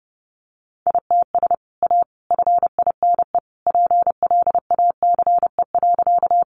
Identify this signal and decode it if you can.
Morse code, heard as a keyed tone, about 30 wpm, 700 Hz. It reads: ITS A FINE PLACE.
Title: Sentences from Top 600 Words 30wpm